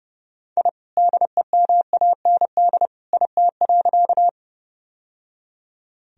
Morse code, heard as a keyed tone, about 30 wpm, 700 Hz. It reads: I DEMAND IT.
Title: Sentences from Top 1000 Words - Review Part 4 - 30wpm